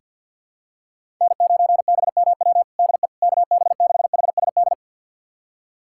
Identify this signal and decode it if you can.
Morse code, heard as a keyed tone, about 51 wpm, 700 Hz. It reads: N9BMW DE KB6HSD
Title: USA Callsign DE Callsign 50wpm